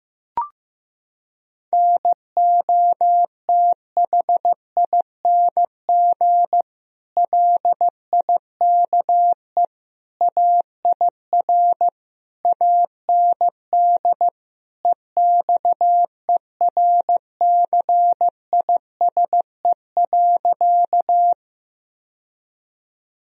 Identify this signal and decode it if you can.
Morse code, heard as a keyed tone, about 15 wpm, 700 Hz. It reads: NOTHING LIKE AIR AND EXERCISE.
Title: Sentences from Top 1000 Words - Review Part 4 - 15wpm